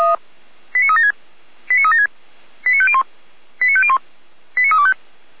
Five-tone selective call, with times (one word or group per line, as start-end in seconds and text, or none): 0.7-1.1 89376
1.7-2.1 89376
2.7-3.0 89501
3.6-4.0 89501
4.6-4.9 89325